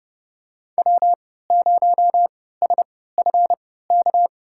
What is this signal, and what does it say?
Morse code, heard as a keyed tone, about 30 wpm, spaced as for 17 wpm, 700 Hz. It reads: W0SFK
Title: USA Call Signs - 5 Characters - 3x Character Spacing - 30wpm